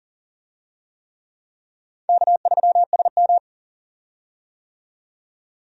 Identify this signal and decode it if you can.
Morse code, heard as a keyed tone, about 40 wpm, 700 Hz. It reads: K3SM
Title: USA Call Signs - 4 Characters - 40wpm